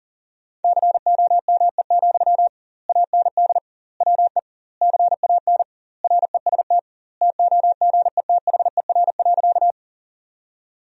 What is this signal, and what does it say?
Morse code, heard as a keyed tone, about 40 wpm, 700 Hz. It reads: COME, AND WE CAN REST TOGETHER.
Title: 2x Word-Spacing - Sentences from Top 500 Words - 40wpm